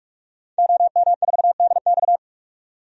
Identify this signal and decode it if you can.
Morse code, heard as a keyed tone, about 46 wpm, 700 Hz. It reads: OM4DX
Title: DXCC-100 Call Signs - 5 Characters - 45wpm